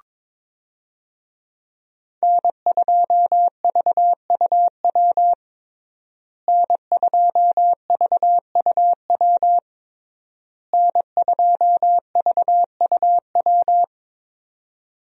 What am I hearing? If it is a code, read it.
Morse code, heard as a keyed tone, about 23 wpm, 700 Hz. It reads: N2VUW N2VUW N2VUW
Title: USA Call Signs - 5 Characters - Repeated 3x - 22wpm